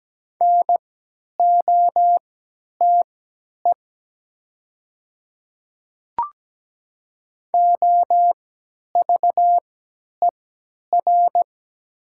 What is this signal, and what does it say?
Morse code, heard as a keyed tone, about 17 wpm, spaced as for 10 wpm, 700 Hz. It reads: NOTE OVER
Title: 3x Character Spacing - Top 500 Words - 17wpm